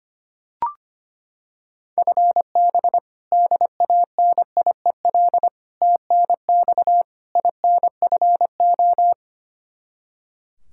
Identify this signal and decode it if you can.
Morse code, heard as a keyed tone, about 25 wpm, 700 Hz. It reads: FB DANIEL TNX INFO
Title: Practice QSO - Exchange 3 - More Info - 25wpm